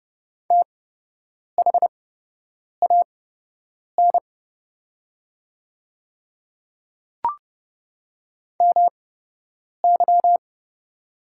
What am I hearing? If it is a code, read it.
Morse code, heard as a keyed tone, about 30 wpm, spaced as for 8 wpm, 700 Hz. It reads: THAN MY
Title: 8x Character Spacing - Top 200 Words - 30wpm